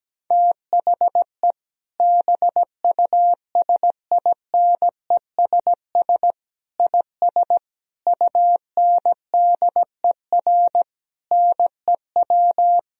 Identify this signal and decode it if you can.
Morse code, heard as a keyed tone, about 17 wpm, 700 Hz. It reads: THE BUSINESS IS UNDER NEW
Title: Sentences from Top 1000 Words - Review Part 4 - 17wpm